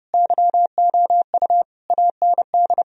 Morse code, heard as a keyed tone, about 30 wpm, 700 Hz. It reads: YOU AND
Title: Sentences from Top 1000 Words - Review Part 3 - 30wpm